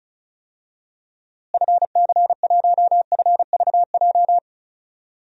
Morse code, heard as a keyed tone, about 34 wpm, 700 Hz. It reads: FC1FVJ
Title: DXCC-100 Call Signs - 6 Characters - Repeated 2x - 35wpm